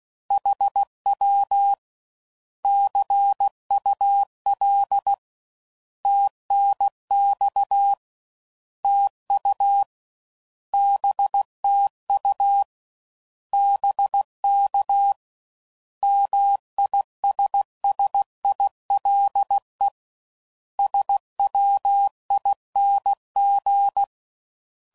Morse code, heard as a keyed tone, about 16 wpm, 800 Hz. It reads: HW CUL TNX TU BTU BK MISSILE SWING
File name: Mixed / Misc / Words